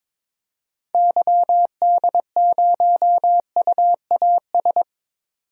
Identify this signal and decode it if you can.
Morse code, heard as a keyed tone, about 22 wpm, 700 Hz. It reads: YD0UAS
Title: DXCC-100 Call Signs - 6 Characters - 22wpm